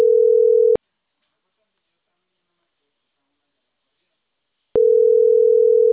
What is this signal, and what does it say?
Ringback tone (North America), 440 + 480 Hz.